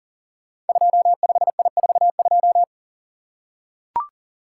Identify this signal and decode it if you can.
Morse code, heard as a keyed tone, about 42 wpm, 690 Hz. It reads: 25I42
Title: Code Groups - 5 Letter:Numbers - 40wpm